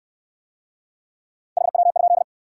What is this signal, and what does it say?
Morse code, heard as a keyed tone, about 65 wpm, 700 Hz. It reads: HW?